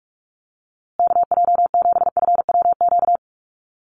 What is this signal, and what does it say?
Morse code, heard as a keyed tone, about 44 wpm, 700 Hz. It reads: KJ7FWQ
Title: USA Call Signs Repeated 2x 45wpm